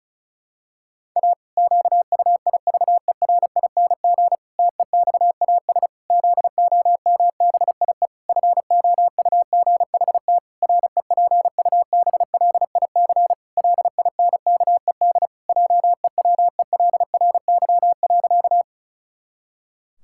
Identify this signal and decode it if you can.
Morse code, heard as a keyed tone, about 36 wpm, 700 Hz. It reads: A QUIVERING TEXAS ZOMBIE FOUGHT REPUBLIC LINKED JEWELRY.
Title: Pangram Challenge v2 - 35wpm